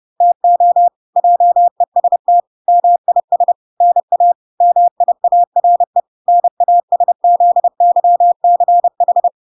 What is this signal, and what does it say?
Morse code, heard as a keyed tone, about 30 wpm, 700 Hz. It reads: TO JEST MIS NA MIARE NASZYCH